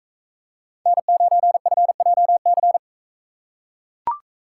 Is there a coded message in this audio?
Yes: Morse code from 0.9 to 2.8 s, heard as a keyed tone, about 43 wpm, 700 Hz. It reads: N9FJC